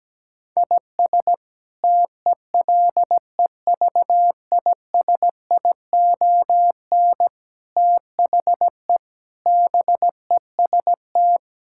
Morse code, heard as a keyed tone, about 17 wpm, 700 Hz. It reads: IS TELEVISION THE BEST
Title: Sentences from Top 1000 Words - Review Part 6 - 17wpm